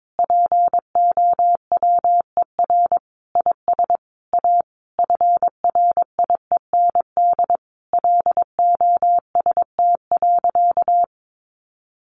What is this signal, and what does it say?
Morse code, heard as a keyed tone, about 22 wpm, 700 Hz. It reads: POWER IS A FRIEND LOST.
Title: Sentences from Top 1400 Words 22wpm